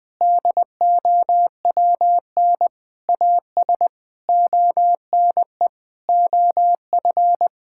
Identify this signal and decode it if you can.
Morse code, heard as a keyed tone, about 21 wpm, 700 Hz. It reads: DOWN AS ONE OF
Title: Sentences with 7-Word Phrases - Spoken Beginning - 20wpm